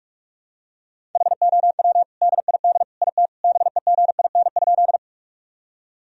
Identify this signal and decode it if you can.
Morse code, heard as a keyed tone, about 46 wpm, 700 Hz. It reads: HOW DID IT BEGIN?
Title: Sentences from Top 1000 Words - Review Part 1 - 45wpm